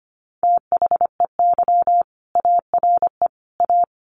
Morse code, heard as a keyed tone, about 25 wpm, 700 Hz. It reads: THEY ARE A
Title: Sentences from Top 1200 Words 25wpm